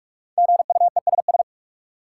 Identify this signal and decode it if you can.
Morse code, heard as a keyed tone, about 46 wpm, 700 Hz. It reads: GUESS
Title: Sentences from Top 1500 Words - New Word Drill - 45wpm